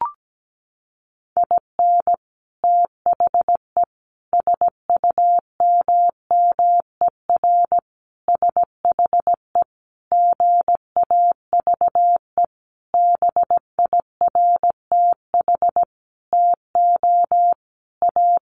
Morse code, heard as a keyed tone, about 17 wpm, 700 Hz. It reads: IN THE SUMMER SHE GAVE BIRTH TO A
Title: Sentences from Top 1100 Words 17wpm